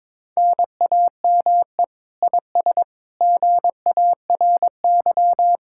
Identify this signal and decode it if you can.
Morse code, heard as a keyed tone, about 22 wpm, 700 Hz. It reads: NAME IS GARY
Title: Speed-Racing Common QSO Phrases Encore - 22wpm